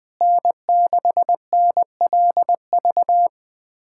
Morse code, heard as a keyed tone, about 20 wpm, 700 Hz. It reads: N6NLV
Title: USA Callsign DE Callsign 20wpm